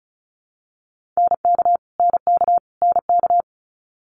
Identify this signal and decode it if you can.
Morse code, heard as a keyed tone, about 36 wpm, 700 Hz. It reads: NK NK NK
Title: Multi-letter Phonemes Repeated 3x 35wpm